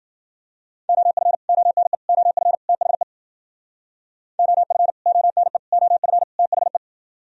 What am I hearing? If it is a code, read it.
Morse code, heard as a keyed tone, about 51 wpm, 700 Hz. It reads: OF ONE OF THE OF ONE OF THE